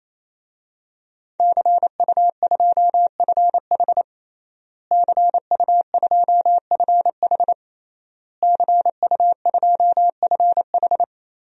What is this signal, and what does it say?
Morse code, heard as a keyed tone, about 27 wpm, 700 Hz. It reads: CU2FH CU2FH CU2FH
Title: DXCC-100 Call Signs - 5 Characters - Repeated 3x - 28wpm